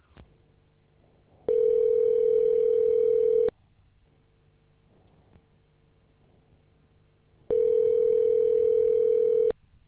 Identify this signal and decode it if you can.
Ringback tone (North America), 440 + 480 Hz.